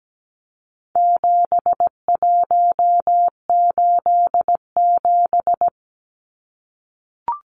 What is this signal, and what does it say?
Morse code, heard as a keyed tone, about 17 wpm, 700 Hz. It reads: 7187
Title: Numbers - 4-Digit 17wpm